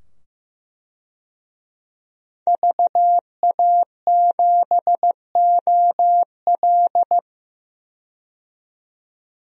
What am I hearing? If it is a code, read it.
Morse code, heard as a keyed tone, about 16 wpm, 700 Hz. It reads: VA7OL
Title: DXCC-100 Call Signs - 5 Characters - 15wpm